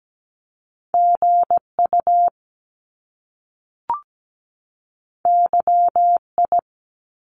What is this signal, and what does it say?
Morse code, heard as a keyed tone, about 17 wpm, 700 Hz. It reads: GU YI
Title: Top Two Letter Combinations in Words 17wpm